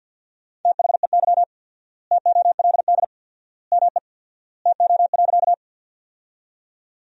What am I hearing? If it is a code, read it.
Morse code, heard as a keyed tone, about 50 wpm, 700 Hz. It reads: THEY TOLD ME TO.
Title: 4x Word-Spacing - Sentences from Top 400 Words - 50wpm